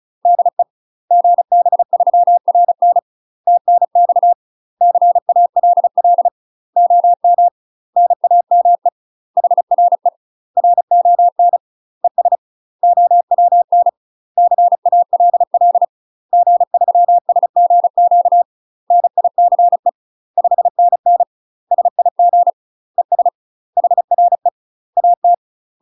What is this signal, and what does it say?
Morse code, heard as a keyed tone, about 34 wpm, 700 Hz. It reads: DE GB3RN TNX CALL OM NAME HRE RON ES OWN CALL G3SGQ NICE 5NN SIG ES HRE AT
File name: UK MASTER QSOs TRAINING / 35WPM / UK MASTER QSOs 0016